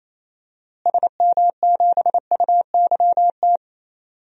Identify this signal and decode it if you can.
Morse code, heard as a keyed tone, about 28 wpm, 700 Hz. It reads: SM7UYT
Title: DXCC-100 Call Signs - 6 Characters - Repeated 2x - 28wpm